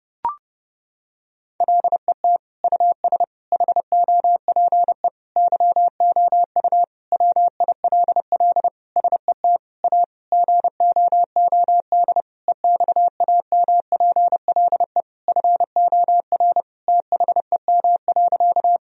Morse code, heard as a keyed tone, about 30 wpm, 700 Hz. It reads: LET US HOPE YOU WILL SET A GOOD EXAMPLE FOR THEM.
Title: Sentences from Top 1000 Words - Review Part 1 - 30wpm